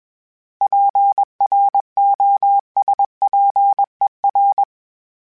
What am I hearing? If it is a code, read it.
Morse code, heard as a keyed tone, about 21 wpm, 800 Hz. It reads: PROSPER
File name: fromSMs / FromSMs mp3s / Cities@20